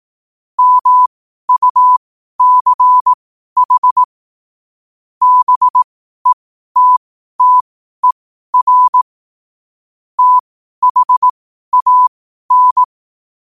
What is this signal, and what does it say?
Morse code, heard as a keyed tone, about 18 wpm, spaced as for 12 wpm, 1000 Hz. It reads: MUCH BETTER THAN
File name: qotdfiles / 12wpm